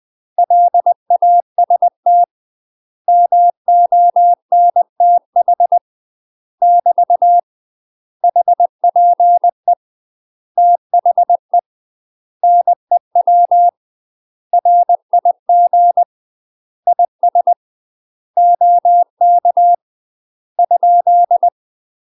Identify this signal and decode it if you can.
Morse code, heard as a keyed tone, about 20 wpm, 700 Hz. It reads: LAST MONTH <BT> HPE THE NEW RIG IS OK ?
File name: UK MASTER QSOs TRAINING / 20WPM / UK MASTER QSOs 0064